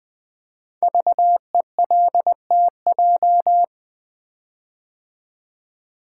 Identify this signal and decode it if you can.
Morse code, heard as a keyed tone, about 20 wpm, 700 Hz. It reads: VELTJ
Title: Code Groups - 5 Letter:Numbers - 20wpm